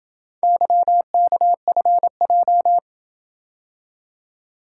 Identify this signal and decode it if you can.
Morse code, heard as a keyed tone, about 26 wpm, 700 Hz. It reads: YKFJ